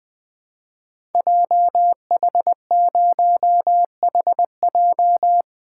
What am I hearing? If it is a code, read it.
Morse code, heard as a keyed tone, about 20 wpm, 700 Hz. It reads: JH0HJ
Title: Code Groups - 5 Letter:Numbers - 20wpm